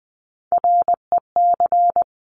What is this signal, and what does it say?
Morse code, heard as a keyed tone, about 20 wpm, 700 Hz. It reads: REC
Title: Top Three Letter Combinations in Words 20wpm